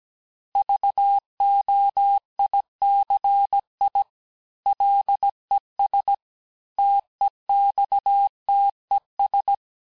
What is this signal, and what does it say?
Morse code, heard as a keyed tone, about 17 wpm, 800 Hz. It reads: VOICI LES TEXTES
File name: cours-du-lundi files / cours-1587-texte-chiffre-vitesse-16MPM